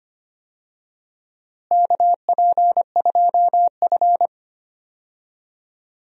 Morse code, heard as a keyed tone, about 25 wpm, 700 Hz. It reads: KP2F
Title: DXCC-100 Call Signs - 4 Characters - Repeated 2x - 25wpm